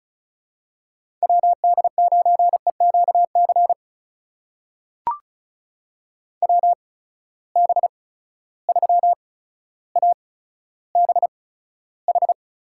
Morse code, heard as a keyed tone, about 36 wpm, spaced as for 15 wpm, 700 Hz. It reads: WD9EQC WB3ABH
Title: USA Call Signs - 6 Characters - 8x Character Spacing - 35wpm